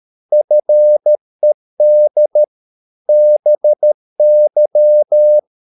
Morse code, heard as a keyed tone, about 13 wpm, 600 Hz. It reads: FED BY